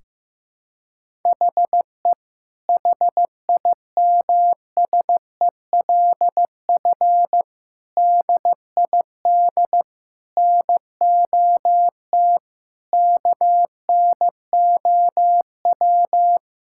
Morse code, heard as a keyed tone, about 15 wpm, 700 Hz. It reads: HE HIMSELF DID NOT KNOW
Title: Sentences from Top 1600 Words 15wpm